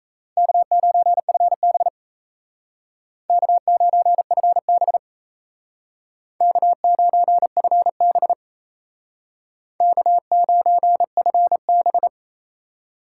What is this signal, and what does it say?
Morse code, heard as a keyed tone, about 35 wpm, 700 Hz. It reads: K9FB K9FB K9FB K9FB